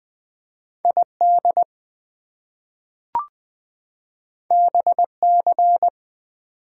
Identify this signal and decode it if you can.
Morse code, heard as a keyed tone, about 20 wpm, 700 Hz. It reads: ID BC
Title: US States and Canadian Provinces 20wpm